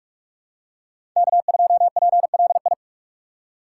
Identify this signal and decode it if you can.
Morse code, heard as a keyed tone, about 46 wpm, 700 Hz. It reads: K2PLI
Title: USA Call Signs - 5 Characters - 3x Character Spacing - 45wpm